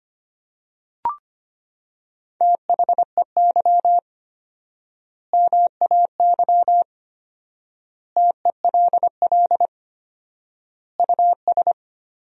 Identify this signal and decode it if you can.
Morse code, heard as a keyed tone, about 25 wpm, 700 Hz. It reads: THEY MAY TELL US
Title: 4x Word-Spacing - Sentences from Top 300 Words - 25wpm